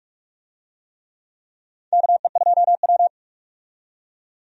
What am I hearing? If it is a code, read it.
Morse code, heard as a keyed tone, about 48 wpm, 700 Hz. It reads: KE2W
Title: USA Call Signs - 4 Characters - 45wpm